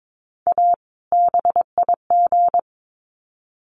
Morse code, heard as a keyed tone, about 22 wpm, 700 Hz. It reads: A BIG
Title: Sets of 2 Words 22wpm